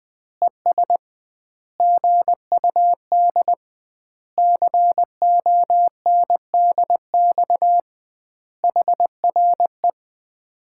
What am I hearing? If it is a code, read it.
Morse code, heard as a keyed tone, about 20 wpm, 700 Hz. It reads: ES GUD CONDX HRE
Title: Common QSO Phrases - 2x Word-Spacing - 20wpm